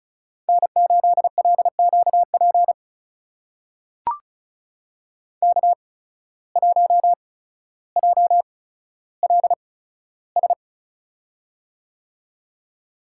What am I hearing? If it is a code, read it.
Morse code, heard as a keyed tone, about 36 wpm, spaced as for 15 wpm, 700 Hz. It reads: N8LQP K1JLS
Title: USA Call Signs - 5 Characters - 8x Character Spacing - 35wpm